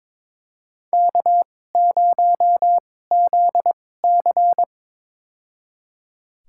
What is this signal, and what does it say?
Morse code, heard as a keyed tone, about 22 wpm, spaced as for 16 wpm, 700 Hz. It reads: K0ZC
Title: USA Call Signs - 4 Characters - 2x Character Spacing - 22wpm